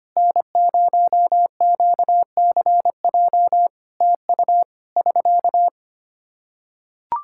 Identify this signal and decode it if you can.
Morse code, heard as a keyed tone, about 25 wpm, 700 Hz. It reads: N0QCJ TU <SK>
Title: Practice QSO - Exchange 4 - Ending - 25wpm